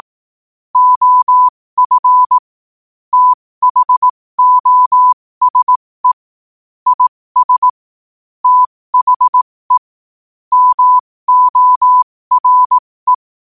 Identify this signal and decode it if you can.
Morse code, heard as a keyed tone, about 18 wpm, spaced as for 15 wpm, 1000 Hz. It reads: OF THOSE IS THE MORE
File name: qotdfiles / 15wpm